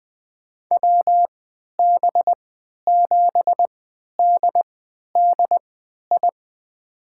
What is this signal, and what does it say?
Morse code, heard as a keyed tone, about 20 wpm, spaced as for 11 wpm, 700 Hz. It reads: WB7DDI
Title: USA Call Signs - 3x Character Spacing - 20wpm